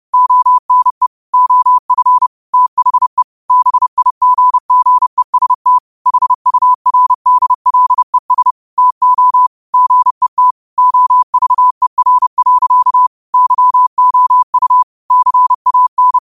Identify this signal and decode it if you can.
Morse code, heard as a keyed tone, about 30 wpm, 1000 Hz. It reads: ONE OF THE BIGGEST HURDLES TO GET OVER. YOU CAN
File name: qotdfiles / 30wpm